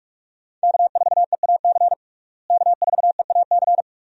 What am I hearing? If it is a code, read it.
Morse code, heard as a keyed tone, about 46 wpm, 700 Hz. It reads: K4EAC K4EAC
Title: USA Call Signs - 5 Characters - Repeated 3x - 45wpm